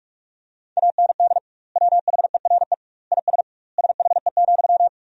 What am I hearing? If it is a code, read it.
Morse code, heard as a keyed tone, about 46 wpm, 700 Hz. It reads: AND WHERE IS SHE,
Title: 2x Word-Spacing - Sentences from Top 200 Words - 45wpm